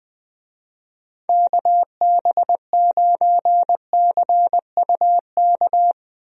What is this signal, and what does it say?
Morse code, heard as a keyed tone, about 20 wpm, 700 Hz. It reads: KB9CUK
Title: USA Call Signs - 6 Characters - 20wpm